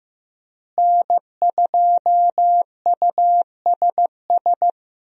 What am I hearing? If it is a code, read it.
Morse code, heard as a keyed tone, about 15 wpm, 700 Hz. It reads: N2USS
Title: USA Call Signs - 5 Characters - Repeated 2x - 15wpm